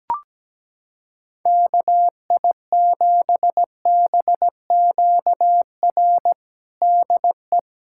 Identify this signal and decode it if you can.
Morse code, heard as a keyed tone, about 17 wpm, 700 Hz. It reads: KI7BQR DE
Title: USA Callsign DE Callsign 17wpm